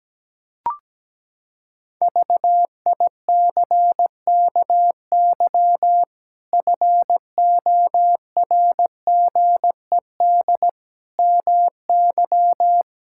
Morse code, heard as a keyed tone, about 17 wpm, 700 Hz. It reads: VICKY FORGED MY
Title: Pangram Challenge v2 - 17wpm